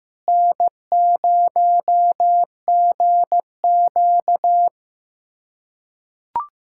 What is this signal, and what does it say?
Morse code, heard as a keyed tone, about 15 wpm, 700 Hz. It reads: N0GQ
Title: USA Callsign DE Callsign - 4 Characters - 15wpm